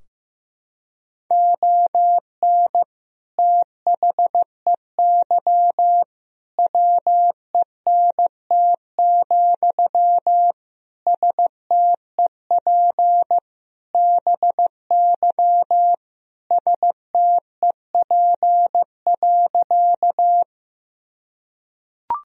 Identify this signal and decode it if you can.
Morse code, heard as a keyed tone, about 15 wpm, 700 Hz. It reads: ON THEY WENT, STEP BY STEP.
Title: Sentences from Top 1000 Words - Review Part 2 - 15wpm